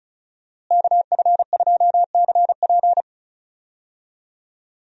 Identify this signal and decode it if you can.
Morse code, heard as a keyed tone, about 34 wpm, 700 Hz. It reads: KF2CP
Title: USA Call Signs - 5 Characters - Repeated 3x - 35wpm